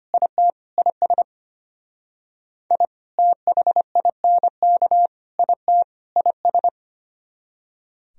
Morse code, heard as a keyed tone, about 31 wpm, 700 Hz. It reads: IT IS I THINK IT IS
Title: Speed-Racing - Sets of 4 Words 25wpm